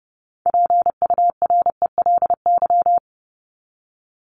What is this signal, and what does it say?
Morse code, heard as a keyed tone, about 30 wpm, 700 Hz. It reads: PURELY
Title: LY Suffix - Top 100 Words with Base Words 30wpm